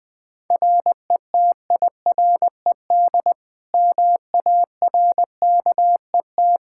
Morse code, heard as a keyed tone, about 20 wpm, 700 Hz. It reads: RETIRED MARKET
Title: Practice QSO - Exchange 3 - More Info - 20wpm